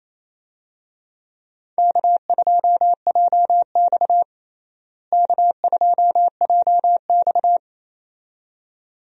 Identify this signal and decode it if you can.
Morse code, heard as a keyed tone, about 28 wpm, 700 Hz. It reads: K2JX K2JX
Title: USA Call Signs - 4 Characters - Repeated 2x - 28wpm